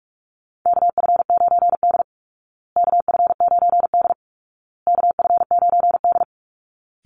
Morse code, heard as a keyed tone, about 44 wpm, 700 Hz. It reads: KF9D KF9D KF9D
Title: Call Signs Repeated 2x - 45wpm